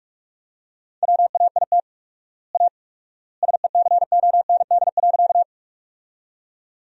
Morse code, heard as a keyed tone, about 42 wpm, 700 Hz. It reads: WAIT A SECOND.
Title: 4x Word-Spacing - Sentences from Top 500 Words - 45wpm